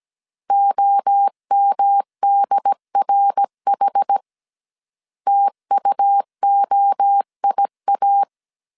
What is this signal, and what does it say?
Morse code, heard as a keyed tone, about 17 wpm, 800 Hz. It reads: OMDRH TUOIA